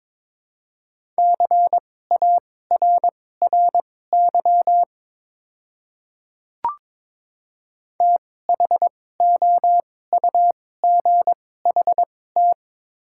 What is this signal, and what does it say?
Morse code, heard as a keyed tone, about 22 wpm, spaced as for 16 wpm, 700 Hz. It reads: CARRY THOUGHT
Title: 2x Character Spacing - Top 300 Words - 22wpm